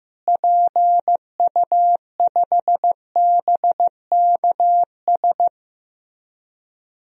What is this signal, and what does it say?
Morse code, heard as a keyed tone, about 16 wpm, 700 Hz. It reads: PU5BKS
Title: DXCC-100 Call Signs - 6 Characters - Repeated 3x - 15wpm